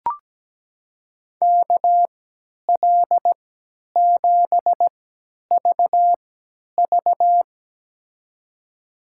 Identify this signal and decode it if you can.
Morse code, heard as a keyed tone, about 17 wpm, spaced as for 10 wpm, 700 Hz. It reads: KL7VV
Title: USA Call Signs - 5 Characters - 3x Character Spacing - 17wpm